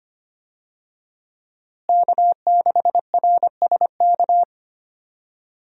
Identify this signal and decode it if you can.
Morse code, heard as a keyed tone, about 25 wpm, 700 Hz. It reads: K6RSK
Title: USA Call Signs - 5 Characters - Repeated 2x - 25wpm